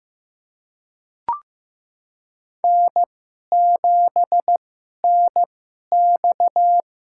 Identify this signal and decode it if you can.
Morse code, heard as a keyed tone, about 15 wpm, spaced as for 11 wpm, 700 Hz. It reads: N7NX
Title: USA Call Signs - 4 Characters - 2x Character Spacing - 15wpm